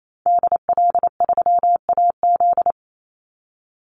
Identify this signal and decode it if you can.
Morse code, heard as a keyed tone, about 28 wpm, 700 Hz. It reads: DL3AZ
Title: Call Signs Repeated 3x - 28wpm